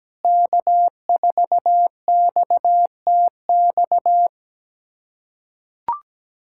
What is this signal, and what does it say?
Morse code, heard as a keyed tone, about 17 wpm, 700 Hz. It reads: K4XTX
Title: USA Callsign DE Callsign - 5 Characters - 17wpm